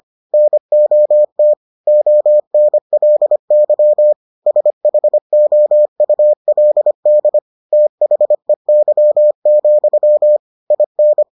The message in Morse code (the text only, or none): NOT ONLY SHOULD THEY, IN